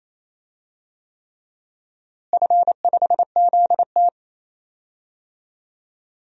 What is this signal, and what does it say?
Morse code, heard as a keyed tone, about 27 wpm, 700 Hz. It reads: F5ZT
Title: DXCC-100 Call Signs - 4 Characters - Repeated 3x - 28wpm